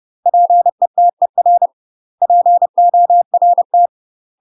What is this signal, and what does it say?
Morse code, heard as a keyed tone, about 30 wpm, 700 Hz. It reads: PETER PORT
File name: UK MASTER QSOs TRAINING / 30WPM / UK MASTER QSOs 0004